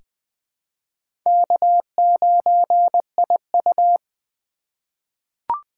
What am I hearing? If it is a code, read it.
Morse code, heard as a keyed tone, about 20 wpm, 700 Hz. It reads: K9IU
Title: USA Call Signs - 4 Characters - 2x Character Spacing - 20wpm